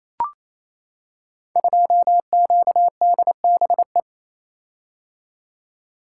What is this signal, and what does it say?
Morse code, heard as a keyed tone, about 27 wpm, 700 Hz. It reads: 2QDBE
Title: Code Groups - 5 Letter:Numbers - 28wpm